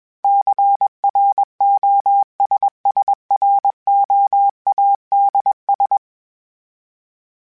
Morse code, heard as a keyed tone, about 21 wpm, 800 Hz. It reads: CROSSROADS
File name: fromSMs / FromSMs mp3s / Cities@20